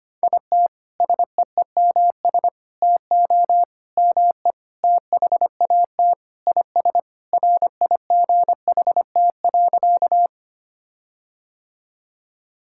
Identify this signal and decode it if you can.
Morse code, heard as a keyed tone, about 25 wpm, 700 Hz. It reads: IT SEEMS TO ME THAT IS RIGHT.
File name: practice / 25wpm / Sentences with 4-Word Phrases - 25wpm